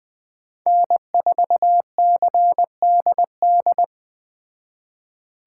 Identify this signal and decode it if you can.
Morse code, heard as a keyed tone, about 21 wpm, 700 Hz. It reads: N4CDD